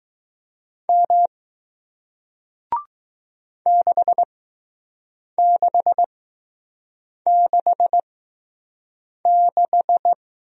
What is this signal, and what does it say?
Morse code, heard as a keyed tone, about 20 wpm, spaced as for 6 wpm, 700 Hz. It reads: M 6666